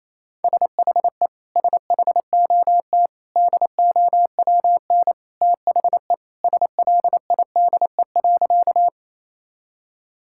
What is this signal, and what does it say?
Morse code, heard as a keyed tone, about 28 wpm, 700 Hz. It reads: SHE SHOT DOWN THE SLIDE.
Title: Sentences from Top 1600 Words 28wpm